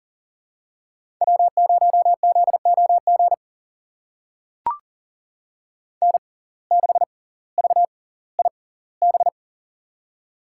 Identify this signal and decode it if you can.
Morse code, heard as a keyed tone, about 40 wpm, spaced as for 20 wpm, 700 Hz. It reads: W0ZOG N6VIB
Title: USA Call Signs - 5 Characters - 6x Character Spacing - 40wpm